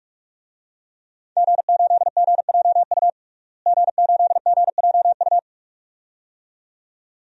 Morse code, heard as a keyed tone, about 46 wpm, 700 Hz. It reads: G8GJU G8GJU
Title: DXCC-100 Call Signs - 5 Characters - Repeated 2x - 45wpm